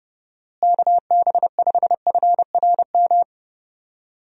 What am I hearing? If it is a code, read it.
Morse code, heard as a keyed tone, about 32 wpm, 700 Hz. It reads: KB5FRM
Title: USA Call Signs - 6 Characters - Repeated 2x - 30wpm